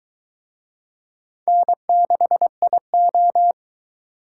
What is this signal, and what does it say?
Morse code, heard as a keyed tone, about 22 wpm, 700 Hz. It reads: N6IO